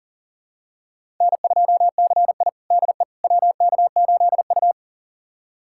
Morse code, heard as a keyed tone, about 40 wpm, 700 Hz. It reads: N2CI DE WK8U
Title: USA Callsign DE Callsign - 4 Characters - 40wpm